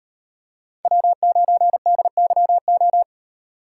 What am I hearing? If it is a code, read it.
Morse code, heard as a keyed tone, about 31 wpm, 700 Hz. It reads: W9DYO
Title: Speed Racing - USA 1x3 Call Signs 25wpm